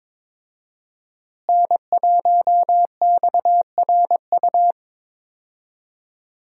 Morse code, heard as a keyed tone, about 22 wpm, 700 Hz. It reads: N1XRU
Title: USA Call Signs - 5 Characters - Repeated 2x - 22wpm